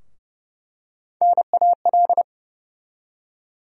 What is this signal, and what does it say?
Morse code, heard as a keyed tone, about 30 wpm, 700 Hz. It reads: NAL
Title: USA Call Sign Postfixes - 3 Letters 30wpm